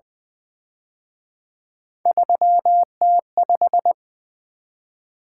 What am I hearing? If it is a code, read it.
Morse code, heard as a keyed tone, about 23 wpm, 700 Hz. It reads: 3T5
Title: Speed-Racing - Three Letter-Number Combinations - 20wpm